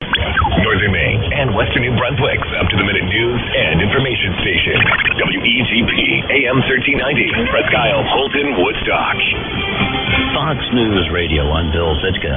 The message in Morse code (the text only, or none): none